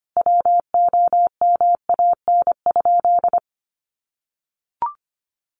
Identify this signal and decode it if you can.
Morse code, heard as a keyed tone, about 25 wpm, 700 Hz. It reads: WOMAN?
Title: Sentences from Top 1100 Words 25wpm